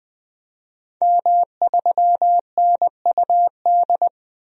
Morse code, heard as a keyed tone, about 20 wpm, 700 Hz. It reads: M3NUD
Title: DXCC-100 Call Signs - 5 Characters - 20wpm